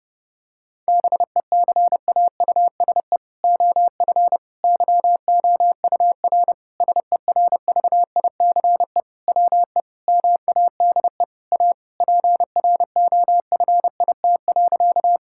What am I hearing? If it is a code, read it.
Morse code, heard as a keyed tone, about 32 wpm, 700 Hz. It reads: BECAUSE OF YOUR SERVICE WE MADE A PROFIT.
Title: Sentences from Top 1000 Words - Review Part 5 - 30wpm